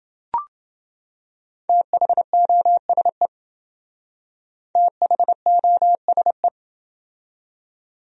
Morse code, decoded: THOSE THOSE

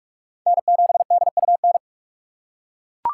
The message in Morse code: N7DUN

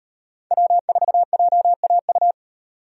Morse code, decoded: W4JAU

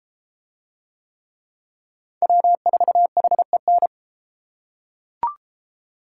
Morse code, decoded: W4HEN

